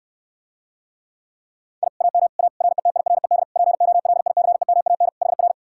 Morse code, heard as a keyed tone, about 69 wpm, 700 Hz. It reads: I AM A RETIRED POLICEMAN HR